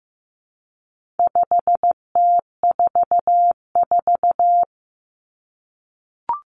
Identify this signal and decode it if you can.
Morse code, heard as a keyed tone, about 15 wpm, 700 Hz. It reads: 5T44
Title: Common Cut Numbers - 4-Digit 15wpm